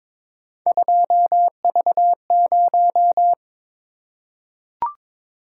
Morse code, heard as a keyed tone, about 22 wpm, 700 Hz. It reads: 2V0
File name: practice / 22wpm / Three Letter-Number Combinations - 22wpm